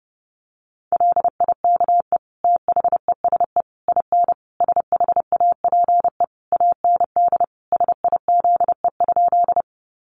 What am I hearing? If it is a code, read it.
Morse code, heard as a keyed tone, about 30 wpm, 700 Hz. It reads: LIKE THESE IN SHAPE AND SIZE?
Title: Sentences from Top 500 Words 30wpm